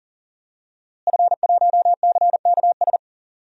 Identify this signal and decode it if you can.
Morse code, heard as a keyed tone, about 37 wpm, 700 Hz. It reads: F1CKS
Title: DXCC-100 Call Signs - 5 Characters - Repeated 3x - 40wpm